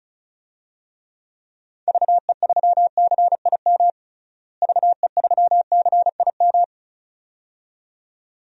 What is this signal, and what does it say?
Morse code, heard as a keyed tone, about 36 wpm, 700 Hz. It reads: VE3CIM VE3CIM
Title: DXCC-100 Call Signs - 6 Characters - Repeated 2x - 35wpm